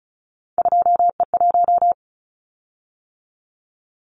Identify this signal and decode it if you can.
Morse code, heard as a keyed tone, about 36 wpm, 700 Hz. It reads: 2E1
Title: Callsign Prefixes for Top 25 DXCC - 35wpm